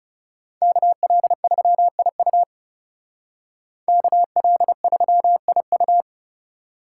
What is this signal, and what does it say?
Morse code, heard as a keyed tone, about 36 wpm, 700 Hz. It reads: KL3IU KL3IU
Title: Speed Racing - USA Call Signs 30wpm